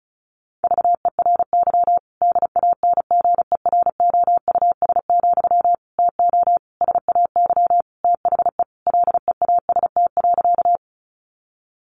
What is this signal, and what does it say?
Morse code, heard as a keyed tone, about 34 wpm, 700 Hz. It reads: VERY DANGEROUS, TO SAY THE LEAST.
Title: Sentences from Top 1200 Words 35wpm